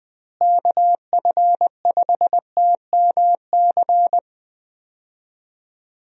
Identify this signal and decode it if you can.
Morse code, heard as a keyed tone, about 20 wpm, 700 Hz. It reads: KF5TMC
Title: USA Call Signs - 6 Characters - Repeated 3x - 20wpm